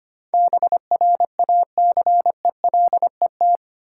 Morse code, heard as a keyed tone, about 30 wpm, 700 Hz. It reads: BRACELET
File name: practice / 25wpm / Difficult Words - Speed-Racing with Hints - 25wpm